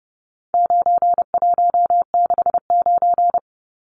Morse code, heard as a keyed tone, about 29 wpm, 700 Hz. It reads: 9169
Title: Numbers - 4-Digit 30wpm